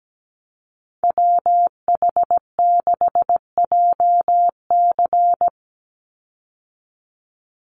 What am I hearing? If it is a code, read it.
Morse code, heard as a keyed tone, about 17 wpm, 700 Hz. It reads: WH6JC
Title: USA Call Signs 17wpm